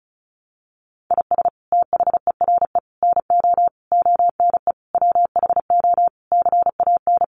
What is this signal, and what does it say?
Morse code, heard as a keyed tone, about 34 wpm, 700 Hz. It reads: IS THERE NO ONE WHO CAN
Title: Sentences from Top 200 Words 35wpm